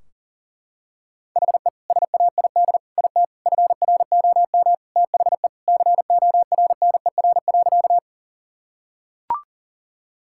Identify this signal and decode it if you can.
Morse code, heard as a keyed tone, about 40 wpm, 700 Hz. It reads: HE SAID IT FROM THE CORNER.
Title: Sentences from Top 1600 Words 40wpm